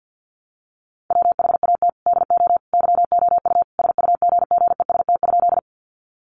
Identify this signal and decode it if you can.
Morse code, heard as a keyed tone, about 50 wpm, 700 Hz. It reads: WHAT DO YOU SUGGEST?
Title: Sentences from Top 1300 Words 50wpm